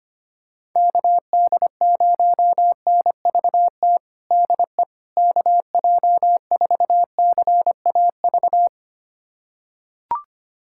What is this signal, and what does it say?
Morse code, heard as a keyed tone, about 25 wpm, 700 Hz. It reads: KD0NVT DE KJ4CAV
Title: USA Callsign DE Callsign 25wpm